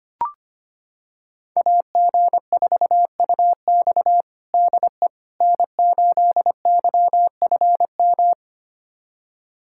Morse code, heard as a keyed tone, about 25 wpm, 700 Hz. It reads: AG4UX DE N8YFM
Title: USA Callsign DE Callsign - 5 Characters - 25wpm